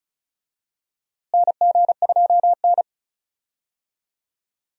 Morse code, heard as a keyed tone, about 36 wpm, 700 Hz. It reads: NG2N
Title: USA Call Signs - 4 Characters - Repeated 2x - 35wpm